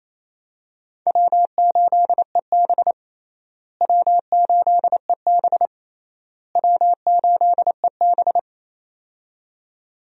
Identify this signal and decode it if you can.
Morse code, heard as a keyed tone, about 28 wpm, 700 Hz. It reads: W8EB W8EB W8EB
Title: USA Call Signs - 4 Characters - Repeated 3x - 28wpm